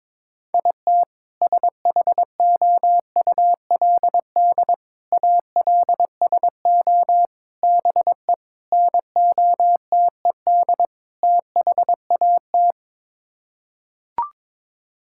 Morse code, heard as a keyed tone, about 22 wpm, 700 Hz. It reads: IT SHOULD ALSO BE NOTED THAT
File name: practice / 22wpm / Sets of 6 Words - 22wpm